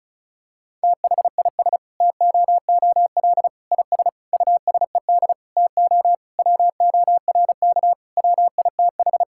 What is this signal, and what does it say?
Morse code, heard as a keyed tone, about 36 wpm, 700 Hz. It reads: THIS TOOL IS USED TO WORK WITH
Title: Sentences from Top 1000 Words - Review Part 6 - 35wpm